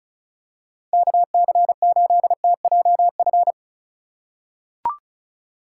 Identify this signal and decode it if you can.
Morse code, heard as a keyed tone, about 36 wpm, 690 Hz. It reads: KC8TJF
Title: USA Call Signs - 6 Characters - 6x Character Spacing - 35wpm